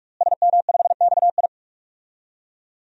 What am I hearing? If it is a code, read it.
Morse code, heard as a keyed tone, about 46 wpm, 700 Hz. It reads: SM5XI
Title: DXCC-100 Call Signs - 5 Characters - Repeated 2x - 45wpm